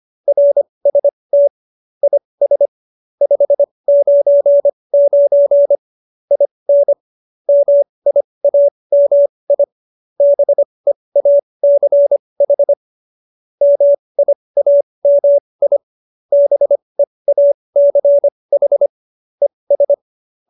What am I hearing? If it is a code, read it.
Morse code, heard as a keyed tone, about 25 wpm, spaced as for 20 wpm, 570 Hz. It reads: RST IS 599 IN MIAMI BEACH MIAMI BEACH ES
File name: QSO317 20